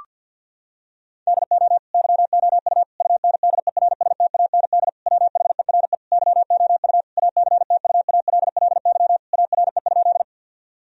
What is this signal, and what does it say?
Morse code, heard as a keyed tone, about 50 wpm, 700 Hz. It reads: DO YOU UNDERSTAND WHERE YOU ACTUALLY ARE?
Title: Sentences from Top 1000 Words - Review Part 4 - 50wpm